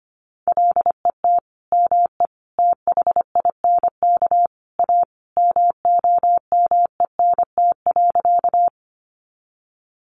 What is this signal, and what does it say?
Morse code, heard as a keyed tone, about 25 wpm, 700 Hz. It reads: LET ME THINK A MOMENT.
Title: Sentences from Top 1000 Words 25wpm